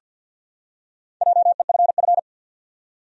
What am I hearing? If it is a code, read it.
Morse code, heard as a keyed tone, about 51 wpm, 700 Hz. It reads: JEFF
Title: USA Common Names - 50wpm